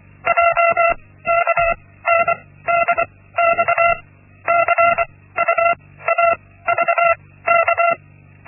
Morse code, heard as a keyed tone, about 24 wpm, spaced as for 16 wpm, 680 Hz. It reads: JKNDX CUAVK
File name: Linksmakalnis-1